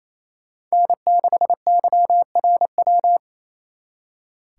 Morse code, heard as a keyed tone, about 34 wpm, 700 Hz. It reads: N6YRW
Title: Speed Racing - USA 1x3 Call Signs 28wpm